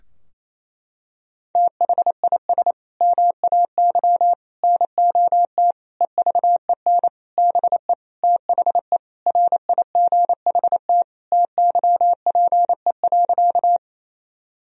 Morse code, heard as a keyed tone, about 28 wpm, 700 Hz. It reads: THIS MAY NOT EVEN BE THE RIGHT TYPE.
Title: Sentences from Top 1000 Words - Review Part 3 - 28wpm